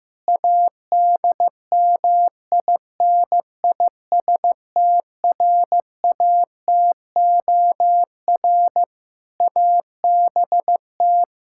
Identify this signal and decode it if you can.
Morse code, heard as a keyed tone, about 15 wpm, 700 Hz. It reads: ADMINISTRATOR ABT
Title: Practice QSO - Exchange 3 - More Info - 15wpm